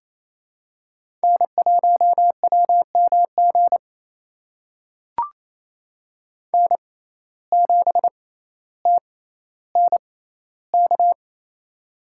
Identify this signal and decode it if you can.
Morse code, heard as a keyed tone, about 28 wpm, spaced as for 14 wpm, 700 Hz. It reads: N1WMG N7TNK
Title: USA Call Signs - 5 Characters - 6x Character Spacing - 28wpm